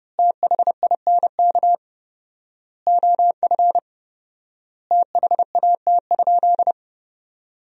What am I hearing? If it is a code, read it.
Morse code, heard as a keyed tone, about 30 wpm, 700 Hz. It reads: THINK OF THAT?
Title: 4x Word-Spacing - Sentences from Top 200 Words - 30wpm